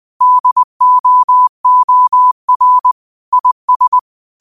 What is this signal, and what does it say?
Morse code, heard as a keyed tone, about 20 wpm, 1000 Hz. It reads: DOOR IS
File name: qotdfiles / 20wpm